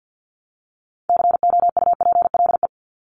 Morse code, heard as a keyed tone, about 50 wpm, 700 Hz. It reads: COUPLE
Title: Top 1001-1250 Words 50wpm